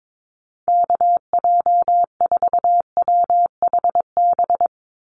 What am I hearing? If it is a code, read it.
Morse code, heard as a keyed tone, about 22 wpm, 700 Hz. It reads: KJ4WHB
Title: USA Call Signs 22wpm